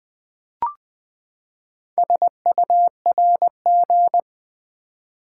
Morse code, heard as a keyed tone, about 20 wpm, 700 Hz. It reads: SURG
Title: Build A Word - 4 Letters At A Time - 8-Letter Words - 20wpm